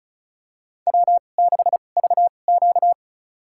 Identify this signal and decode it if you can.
Morse code, heard as a keyed tone, about 34 wpm, spaced as for 25 wpm, 700 Hz. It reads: W6VQ
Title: USA Call Signs - 4 Characters - 2x Character Spacing - 35wpm